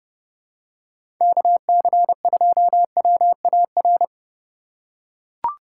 Morse code, heard as a keyed tone, about 32 wpm, 700 Hz. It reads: KC2WAR
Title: USA Call Signs - 8x Character Spacing - 30wpm